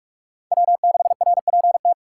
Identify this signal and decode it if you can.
Morse code, heard as a keyed tone, about 59 wpm, spaced as for 52 wpm, 700 Hz. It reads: W6RPT
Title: Speed Racing - USA 1x3 Call Signs 45wpm